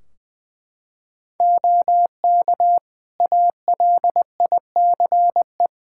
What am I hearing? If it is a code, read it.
Morse code, heard as a keyed tone, about 20 wpm, 700 Hz. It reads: OK ALICE
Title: Practice QSO - Exchange 3 - More Info - 20wpm